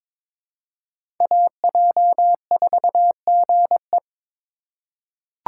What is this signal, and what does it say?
Morse code, heard as a keyed tone, about 21 wpm, 700 Hz. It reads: AJ4GE
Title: USA Call Signs - 2x Character Spacing - 22wpm